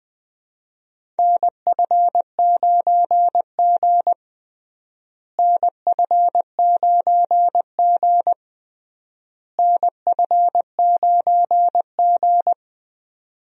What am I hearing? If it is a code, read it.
Morse code, heard as a keyed tone, about 21 wpm, 700 Hz. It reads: NF9G NF9G NF9G
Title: USA Call Signs - 4 Characters - Repeated 3x - 20wpm